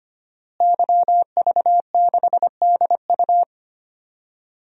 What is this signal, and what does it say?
Morse code, heard as a keyed tone, about 25 wpm, 700 Hz. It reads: YV6DU
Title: DXCC-100 Call Signs - 5 Characters - Repeated 3x - 25wpm